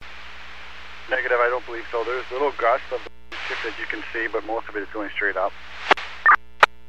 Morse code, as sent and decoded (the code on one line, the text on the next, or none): none